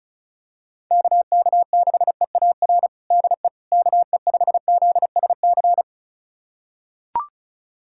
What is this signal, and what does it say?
Morse code, heard as a keyed tone, about 36 wpm, 700 Hz. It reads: KK6EAR DE KE5ZSC
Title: USA Callsign DE Callsign 35wpm